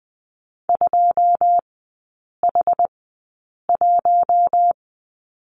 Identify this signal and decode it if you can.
Morse code, heard as a keyed tone, about 20 wpm, spaced as for 8 wpm, 700 Hz. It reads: 2H1
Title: 100 Lesson 17e - Letter u - Three Characters 20wpm